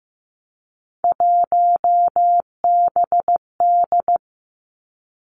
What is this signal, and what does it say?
Morse code, heard as a keyed tone, about 15 wpm, 700 Hz. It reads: 1BD
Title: Three Challenging Letter-Number Combinations 15wpm